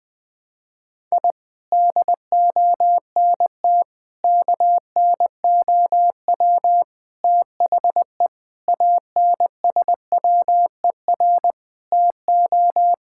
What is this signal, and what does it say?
Morse code, heard as a keyed tone, about 21 wpm, 700 Hz. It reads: I DONT KNOW THE ANSWER TO
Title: Sentences with 7-Word Phrases - 20wpm